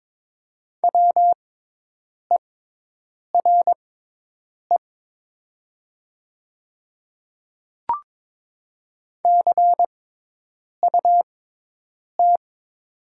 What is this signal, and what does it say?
Morse code, heard as a keyed tone, about 22 wpm, spaced as for 8 wpm, 700 Hz. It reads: WERE CUT
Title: 6x Character Spacing - Top 300 Words - 22wpm